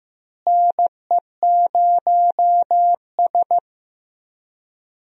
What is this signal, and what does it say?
Morse code, heard as a keyed tone, about 15 wpm, 700 Hz. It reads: NE0S
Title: USA Call Signs - 4 Characters - Repeated 2x - 15wpm